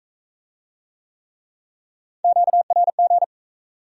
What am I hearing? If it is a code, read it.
Morse code, heard as a keyed tone, about 40 wpm, 700 Hz. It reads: QRG